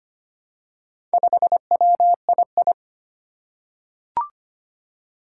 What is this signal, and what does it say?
Morse code, heard as a keyed tone, about 25 wpm, 700 Hz. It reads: 5WII
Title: Four Challenging Letter-Number Combinations 25wpm